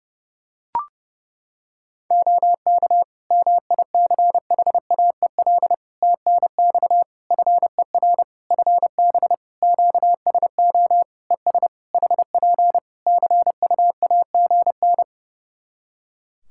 Morse code, heard as a keyed tone, about 30 wpm, 700 Hz. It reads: OK MICHAEL TNX FER FB QSO ES HP CUAGN
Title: K1USN SST QSO Practice 30wpm